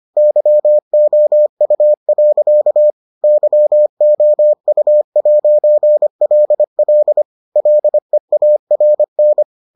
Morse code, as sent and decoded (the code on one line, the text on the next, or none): -.-- --- ..- .-.-.- / -.-- --- ..- .----. .-.. .-.. / .-.. .- .-. -.
YOU. YOU'LL LEARN